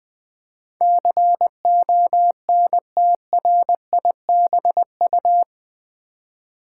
Morse code, heard as a keyed tone, about 20 wpm, 700 Hz. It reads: CONTRIBU
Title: Build A Word - 4 Letters At A Time - 11-Letter Words - 20wpm